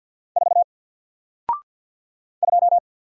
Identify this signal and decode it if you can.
Morse code, heard as a keyed tone, about 48 wpm, 700 Hz. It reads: 4 2
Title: Single Letter-Number Alt 50wpm